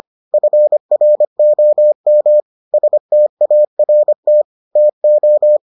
Morse code, heard as a keyed tone, about 25 wpm, 600 Hz. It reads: FROM START TO